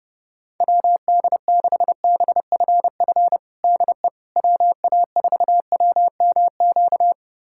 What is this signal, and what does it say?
Morse code, heard as a keyed tone, about 30 wpm, 700 Hz. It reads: WD6BFF DE WA4WMQ
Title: USA Callsign DE Callsign - 6 Characters - 30wpm